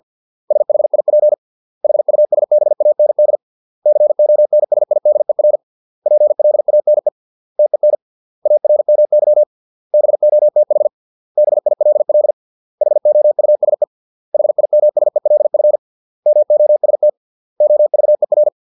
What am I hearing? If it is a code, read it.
Morse code, heard as a keyed tone, about 50 wpm, 600 Hz. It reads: SHIP HUSBAND CONSIDER PLANE TEN ARMY BOTH BILL HOUSE HIMSELF MOST OVER